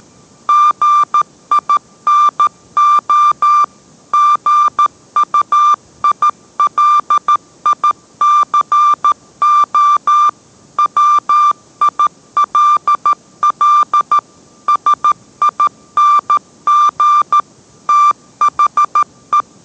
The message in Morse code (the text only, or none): GINO GUILICO WILL SING THE